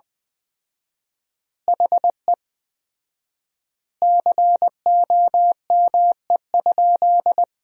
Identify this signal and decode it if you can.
Morse code, heard as a keyed tone, about 21 wpm, 700 Hz. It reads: HE COME?
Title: 4x Word-Spacing - Sentences from Top 200 Words - 20wpm